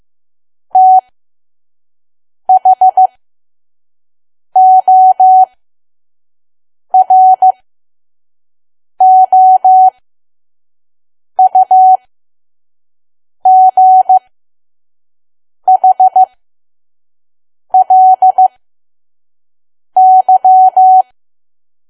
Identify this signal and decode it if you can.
Morse code, heard as a keyed tone, about 15 wpm, spaced as for 5 wpm, 750 Hz. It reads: THOROUGHLY
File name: Archive / 5 WPM files / PM